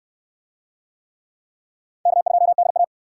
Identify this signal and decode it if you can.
Morse code, heard as a keyed tone, about 63 wpm, 700 Hz. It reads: K2DA